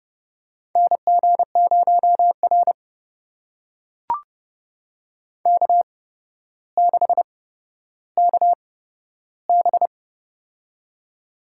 Morse code, heard as a keyed tone, about 30 wpm, spaced as for 13 wpm, 700 Hz. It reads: NG0R K6KB
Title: USA Call Signs - 8x Character Spacing - 30wpm